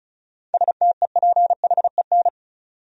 Morse code, heard as a keyed tone, about 34 wpm, 700 Hz. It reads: STEPHEN
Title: USA Common Names - 35wpm